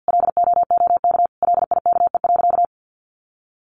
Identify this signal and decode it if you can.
Morse code, heard as a keyed tone, about 49 wpm, 700 Hz. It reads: LOOK LIKE.
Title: Sentences from Top 1300 Words 50wpm